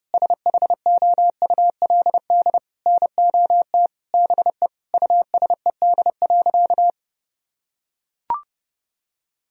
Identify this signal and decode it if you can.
Morse code, heard as a keyed tone, about 30 wpm, 700 Hz. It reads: SHOULD NOT BE USED.
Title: Sentences from Top 1000 Words - Review Part 4 - 30wpm